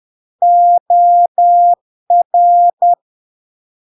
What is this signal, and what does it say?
Morse code, heard as a keyed tone, about 10 wpm, 700 Hz. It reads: OR
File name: LARK-FD-2022-Log-in-CW-10wpm-700-Hz